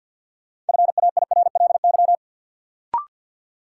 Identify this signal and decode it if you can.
Morse code, heard as a keyed tone, about 48 wpm, 700 Hz. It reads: FAIRLY
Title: Sentences from Top 1100 Words - New Word Drill Alt - 50wpm